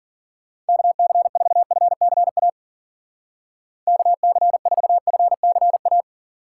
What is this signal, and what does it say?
Morse code, heard as a keyed tone, about 50 wpm, 700 Hz. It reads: KC4FCA KC4FCA
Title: Speed Racing - USA 2x3 Call Signs 40wpm